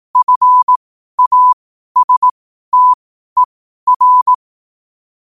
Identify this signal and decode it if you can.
Morse code, heard as a keyed tone, about 18 wpm, spaced as for 12 wpm, 1000 Hz. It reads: FASTER
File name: qotdfiles / 12wpm